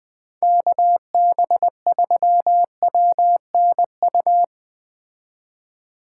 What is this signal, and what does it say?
Morse code, heard as a keyed tone, about 20 wpm, 700 Hz. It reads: KB3WNU
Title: Speed Racing - USA 2x3 Call Signs 17wpm